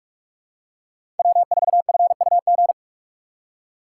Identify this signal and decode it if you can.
Morse code, heard as a keyed tone, about 44 wpm, 700 Hz. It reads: W4FUG
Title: USA Call Signs - 5 Characters - 4x Character Spacing - 45wpm